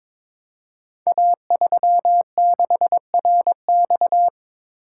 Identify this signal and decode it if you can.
Morse code, heard as a keyed tone, about 23 wpm, 700 Hz. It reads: A36RX
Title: Code Groups - 5 Letter:Numbers - 22wpm